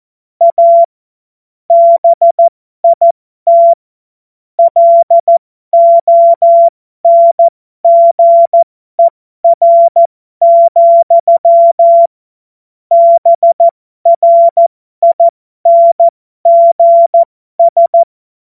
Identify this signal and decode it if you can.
Morse code, heard as a keyed tone, about 14 wpm, spaced as for 12 wpm, 680 Hz. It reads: A BIT LONGER, BRINGS